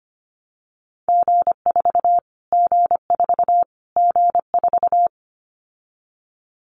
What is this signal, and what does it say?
Morse code, heard as a keyed tone, about 25 wpm, 700 Hz. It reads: G4 G4 G4
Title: Callsign Prefixes for Top 25 DXCC - Repeated 3x - 25wpm